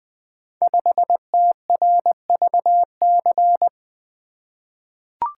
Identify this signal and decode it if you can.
Morse code, heard as a keyed tone, about 20 wpm, 700 Hz. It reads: 5TRVC
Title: Code Groups - 5 Letter:Numbers - 20wpm